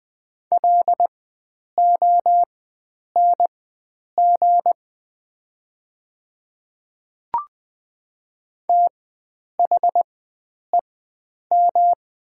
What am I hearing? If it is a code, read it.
Morse code, heard as a keyed tone, about 20 wpm, spaced as for 9 wpm, 700 Hz. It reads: LONG THEM
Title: 4x Character Spacing - Top 100 Words - 20wpm